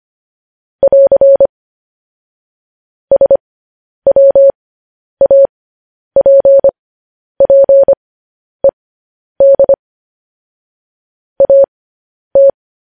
Morse code, heard as a keyed tone, about 25 wpm, spaced as for 10 wpm, 550 Hz. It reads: <AR> SWAPPED AT
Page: news headlines in morse code at 10 wpm